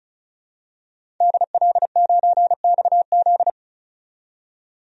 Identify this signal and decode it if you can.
Morse code, heard as a keyed tone, about 36 wpm, 700 Hz. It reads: DL9XZ
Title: DXCC-100 Call Signs - 5 Characters - 35wpm